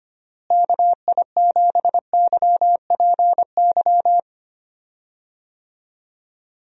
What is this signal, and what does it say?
Morse code, heard as a keyed tone, about 25 wpm, 700 Hz. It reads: KI7YPY
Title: USA Call Signs - 6 Characters - Repeated 2x - 25wpm